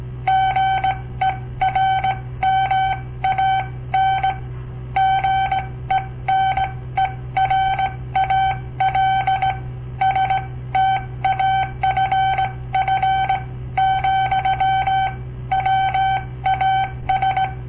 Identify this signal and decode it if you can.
Morse code, heard as a keyed tone, about 18 wpm, spaced as for 15 wpm, 770 Hz. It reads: GERMAN GENERAL STAFF, WAS
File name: papertapes / local / Code Practice-01